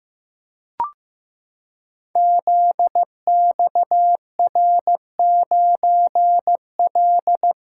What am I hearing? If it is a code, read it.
Morse code, heard as a keyed tone, about 15 wpm, 700 Hz. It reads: ZXR9L
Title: Code Groups - 5 Letter:Numbers - 15wpm